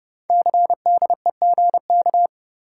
Morse code, heard as a keyed tone, about 30 wpm, 700 Hz. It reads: CDEGK
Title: Code Groups - 5 Letters - 30wpm